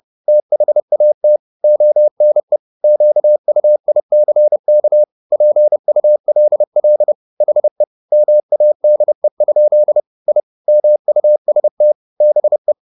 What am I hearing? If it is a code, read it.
Morse code, heard as a keyed tone, about 30 wpm, 600 Hz. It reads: THAT ONE QUICK PULL HE MADE? I MUST BE
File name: the old man and the sea / the old man and the sea 1 30wpm